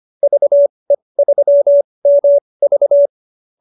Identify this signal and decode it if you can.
Morse code, heard as a keyed tone, about 25 wpm, spaced as for 20 wpm, 590 Hz. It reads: VE3MV
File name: QSO303 20